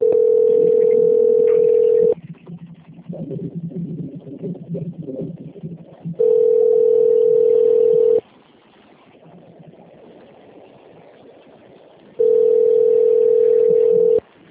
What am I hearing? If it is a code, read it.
Ringback tone (North America), 440 + 480 Hz.
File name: out- - - - - .36875